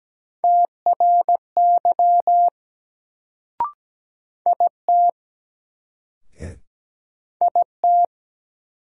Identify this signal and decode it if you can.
Morse code, heard as a keyed tone, about 17 wpm, 700 Hz. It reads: TRY IT IT
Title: Sets of 2 Words - New Word Drill - 17wpm